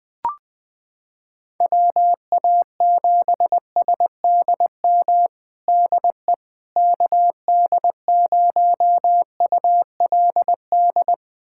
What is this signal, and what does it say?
Morse code, heard as a keyed tone, about 20 wpm, 700 Hz. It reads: WA7SDM DE KD0ULD
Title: USA Callsign DE Callsign - 6 Characters - 20wpm